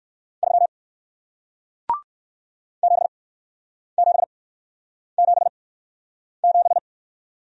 Morse code, heard as a keyed tone, about 56 wpm, 700 Hz. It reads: 3 7 7 7 7
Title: Speed-Racing Single Letter-Number 45wpm